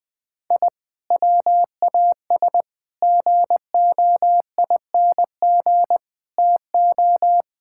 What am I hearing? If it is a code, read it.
Morse code, heard as a keyed tone, about 20 wpm, 700 Hz. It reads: I WAS GOING TO
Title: Sentences with 5-Word Phrases - Spoken Beginning - 20wpm